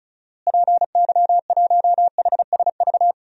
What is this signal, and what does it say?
Morse code, heard as a keyed tone, about 36 wpm, 700 Hz. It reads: PY1HSV
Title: DXCC-100 Call Signs - 6 Characters - Repeated 2x - 35wpm